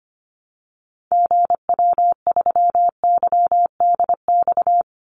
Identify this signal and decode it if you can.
Morse code, heard as a keyed tone, about 25 wpm, 700 Hz. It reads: GW3YDX
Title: Call Signs - 25wpm